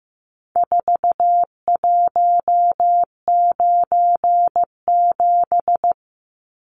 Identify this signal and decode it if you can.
Morse code, heard as a keyed tone, about 15 wpm, 700 Hz. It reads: 4197
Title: Numbers - 4-Digit 15wpm